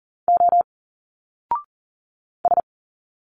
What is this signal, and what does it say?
Morse code, heard as a keyed tone, about 40 wpm, 700 Hz. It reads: O S
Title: Single Letters 40wpm